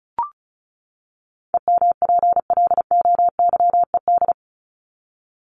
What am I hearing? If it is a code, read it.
Morse code, heard as a keyed tone, about 34 wpm, 700 Hz. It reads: EMPLOYED
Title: ED Suffix - Top 100 Words with Base Words 35wpm